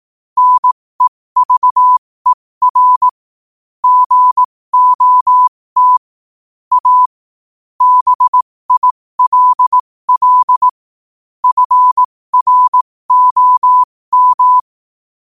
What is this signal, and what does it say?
Morse code, heard as a keyed tone, about 18 wpm, spaced as for 15 wpm, 1000 Hz. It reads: NEVER GOT A BILL FROM
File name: qotdfiles / 15wpm